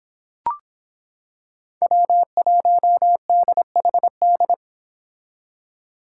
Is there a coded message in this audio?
Yes: Morse code from 1.8 to 4.5 s, heard as a keyed tone, about 25 wpm, 700 Hz. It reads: W1DHD